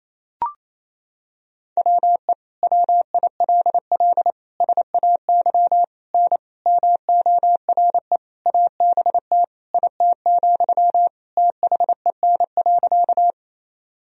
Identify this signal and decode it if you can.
Morse code, heard as a keyed tone, about 28 wpm, 700 Hz. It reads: WE WILL SAY N MORE ABT IT, THEN.
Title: Sentences from Top 200 Words and Abbreviations 28wpm